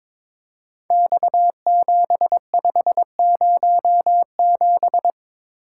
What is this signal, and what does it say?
Morse code, heard as a keyed tone, about 23 wpm, 700 Hz. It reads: X7507
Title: Code Groups - 5 Letter:Numbers - 22wpm